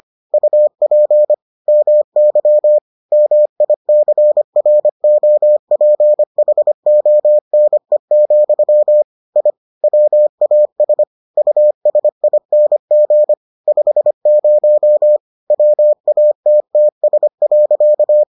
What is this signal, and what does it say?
Morse code, heard as a keyed tone, about 25 wpm, 600 Hz. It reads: UP MY MICROPHONE, I WAS USING 50 WATTS.